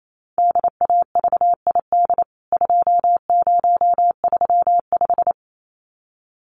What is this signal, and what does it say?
Morse code, heard as a keyed tone, about 28 wpm, 700 Hz. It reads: DAVID 2035
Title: CWT Exchange 28wpm